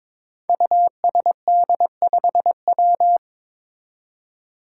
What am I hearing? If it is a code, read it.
Morse code, heard as a keyed tone, about 22 wpm, 700 Hz. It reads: USD5W
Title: Five Challenging Letter-Number Combinations 22wpm